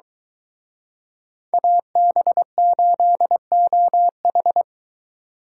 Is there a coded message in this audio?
Yes: Morse code from 1.5 to 4.6 s, heard as a keyed tone, about 23 wpm, 700 Hz. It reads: AB8OH